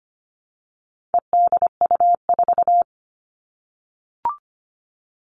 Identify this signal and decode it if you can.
Morse code, heard as a keyed tone, about 25 wpm, 700 Hz. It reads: EDU4
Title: All Cut Numbers - 4-Digit 25wpm